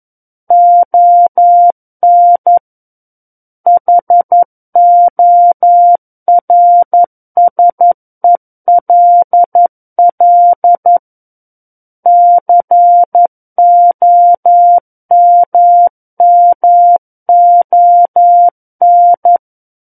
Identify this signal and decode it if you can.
Morse code, heard as a keyed tone, about 11 wpm, 700 Hz. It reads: ON HORSELL COMMON
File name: 11 WPM